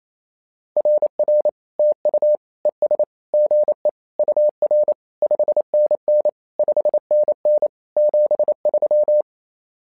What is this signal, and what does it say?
Morse code, heard as a keyed tone, about 28 wpm, 600 Hz. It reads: RR TU ES GE UR 5NN 5NN 73
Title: SOTA QSO Practice 28wpm